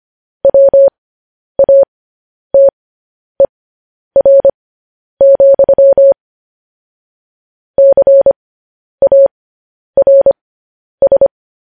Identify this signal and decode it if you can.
Morse code, heard as a keyed tone, about 25 wpm, spaced as for 10 wpm, 550 Hz. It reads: WATER, CARS
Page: news-headlines-in-morse-code-at-10-wpm